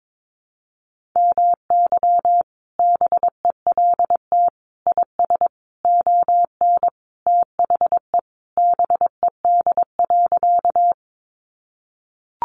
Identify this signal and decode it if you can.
Morse code, heard as a keyed tone, about 22 wpm, 700 Hz. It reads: MY BELT IS ON THE BED.
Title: Sentences from Top 1400 Words 22wpm